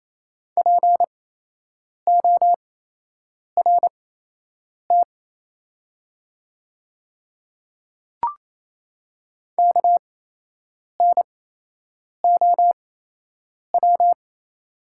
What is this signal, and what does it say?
Morse code, heard as a keyed tone, about 28 wpm, spaced as for 8 wpm, 700 Hz. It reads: PORT KNOW
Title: 8x Character Spacing - Top 300 Words - 28wpm